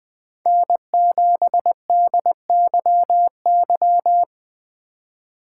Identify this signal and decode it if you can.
Morse code, heard as a keyed tone, about 21 wpm, 700 Hz. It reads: N7DYY
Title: USA Call Signs - 5 Characters - 4x Character Spacing - 20wpm